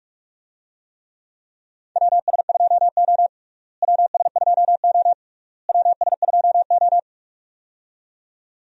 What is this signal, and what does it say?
Morse code, heard as a keyed tone, about 41 wpm, 700 Hz. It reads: WS2O WS2O WS2O
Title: USA Call Signs - 4 Characters - Repeated 3x - 45wpm